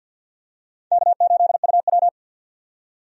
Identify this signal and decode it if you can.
Morse code, heard as a keyed tone, about 52 wpm, 700 Hz. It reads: K8UW
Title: USA Call Signs - 4 Characters - 50wpm